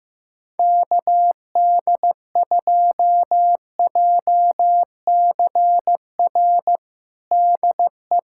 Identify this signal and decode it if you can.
Morse code, heard as a keyed tone, about 16 wpm, 700 Hz. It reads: KD2JCR DE
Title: USA Callsign DE Callsign 15wpm